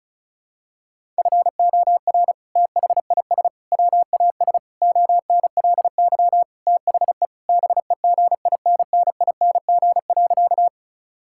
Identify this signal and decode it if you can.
Morse code, heard as a keyed tone, about 34 wpm, 690 Hz. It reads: FOR THIS WAS ONLY THE BEGINNING.
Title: Sentences from Top 1000 Words - Review Part 5 - 35wpm